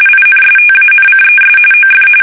RTTY, about 45.45 baud. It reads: INSTANCES OF
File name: fsk850 / usnavy2